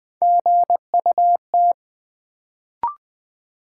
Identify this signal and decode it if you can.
Morse code, heard as a keyed tone, about 20 wpm, 700 Hz. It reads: GUT
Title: Build A Word - 1 Letter At A Time - 3-Letter Words - 20wpm